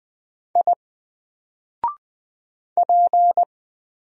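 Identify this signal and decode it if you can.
Morse code, heard as a keyed tone, about 20 wpm, 700 Hz. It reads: I P
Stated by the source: Single letter and number Morse drill at 20 wpm, from N4LCD.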